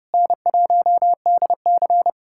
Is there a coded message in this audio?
Yes: Morse code from 0.1 to 2.1 s, heard as a keyed tone, about 34 wpm, 700 Hz. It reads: N1DC